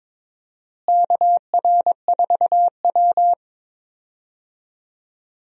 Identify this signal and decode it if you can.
Morse code, heard as a keyed tone, about 22 wpm, 700 Hz. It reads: KR4W
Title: USA Call Signs - 4 Characters - Repeated 3x - 22wpm